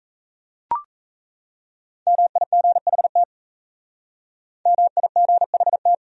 Morse code, heard as a keyed tone, about 34 wpm, 700 Hz. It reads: MIGHT MIGHT
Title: Speed-Racing Top 500 Words 28wpm